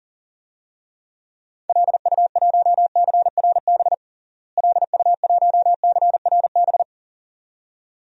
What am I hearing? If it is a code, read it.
Morse code, heard as a keyed tone, about 40 wpm, 700 Hz. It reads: LU1CRB LU1CRB
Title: DXCC-100 Call Signs - 6 Characters - Repeated 2x - 40wpm